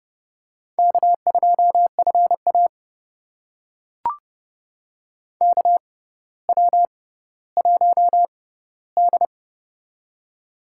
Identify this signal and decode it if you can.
Morse code, heard as a keyed tone, about 30 wpm, spaced as for 15 wpm, 700 Hz. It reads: K2FA KW1D
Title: USA Call Signs - 6x Character Spacing - 30wpm